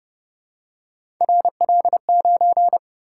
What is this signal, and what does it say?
Morse code, heard as a keyed tone, about 28 wpm, 700 Hz. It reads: RL9